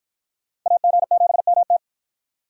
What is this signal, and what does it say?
Morse code, heard as a keyed tone, about 45 wpm, 700 Hz. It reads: AG7MT